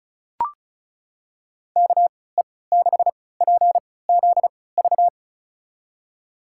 Morse code, heard as a keyed tone, about 34 wpm, spaced as for 20 wpm, 700 Hz. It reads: KE6PZV
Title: USA Call Signs - 3x Character Spacing - 35wpm